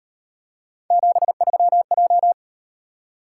Morse code, heard as a keyed tone, about 36 wpm, 700 Hz. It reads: 73J